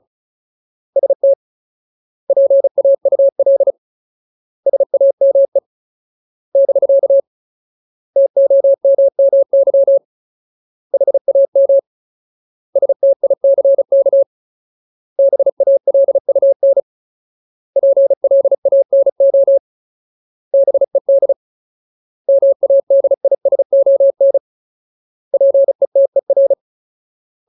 Morse code, heard as a keyed tone, about 36 wpm, 550 Hz. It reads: ST PAUL SAME <BK> TOMMY HAM STICK BALUN PLANO BED MADISON PETER
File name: Ultimate Word List / 35WPM Ultimate